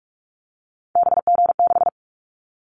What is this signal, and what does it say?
Morse code, heard as a keyed tone, about 44 wpm, 700 Hz. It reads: BG6
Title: Callsign Prefixes for Top 100 DXCC - 45wpm